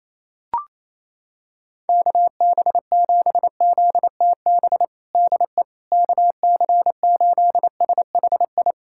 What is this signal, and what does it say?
Morse code, heard as a keyed tone, about 28 wpm, 700 Hz. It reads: KB7ZTB DE KC8SHI
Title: USA Callsign DE Callsign - 6 Characters - 28wpm